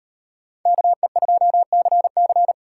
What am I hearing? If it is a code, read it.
Morse code, heard as a keyed tone, about 38 wpm, 700 Hz. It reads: KE2CC